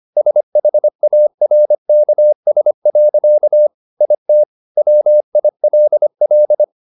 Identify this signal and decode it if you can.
Morse code, heard as a keyed tone, about 25 wpm, 600 Hz. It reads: SHARKS. IT WILL